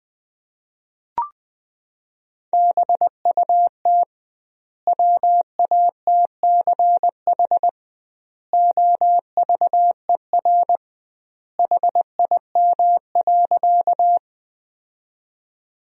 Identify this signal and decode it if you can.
Morse code, heard as a keyed tone, about 21 wpm, 700 Hz. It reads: BUT WATCH OVER HIM.
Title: 2x Word-Spacing - Sentences from Top 300 Words - 20wpm